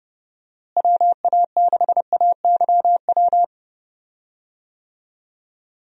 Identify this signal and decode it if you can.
Morse code, heard as a keyed tone, about 30 wpm, 700 Hz. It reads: WA6AYW
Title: USA Call Signs - 6 Characters - Repeated 2x - 30wpm